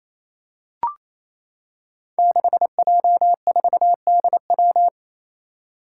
Morse code, heard as a keyed tone, about 28 wpm, 700 Hz. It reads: 6J4DW
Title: Five Challenging Letter-Number Combinations 28wpm